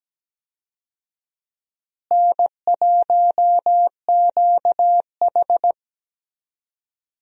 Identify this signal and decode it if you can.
Morse code, heard as a keyed tone, about 17 wpm, 700 Hz. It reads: N1QH
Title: USA Call Signs - 4 Characters - Repeated 3x - 17wpm